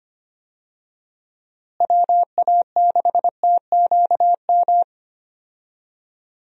Morse code, heard as a keyed tone, about 25 wpm, 700 Hz. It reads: WA6TQM
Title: USA Call Signs - 6 Characters - Repeated 2x - 25wpm